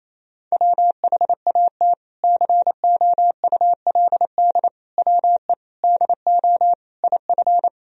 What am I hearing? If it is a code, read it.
Morse code, heard as a keyed tone, about 28 wpm, 700 Hz. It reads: WHAT COULD WE DO IF
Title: Sentences from Top 400 Words and Abbreviations 28wpm